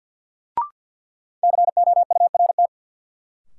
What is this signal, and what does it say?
Morse code, heard as a keyed tone, about 51 wpm, 700 Hz. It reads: COURT
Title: Sentences from Top 1600 Words - New Word Drill - 50wpm